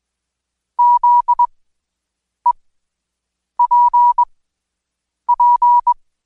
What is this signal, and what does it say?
Morse code, heard as a keyed tone, about 21 wpm, spaced as for 7 wpm, 970 Hz. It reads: ZEPP
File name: Code Quick / AudioDownld2 / Seg113-Day18-DayDrill 18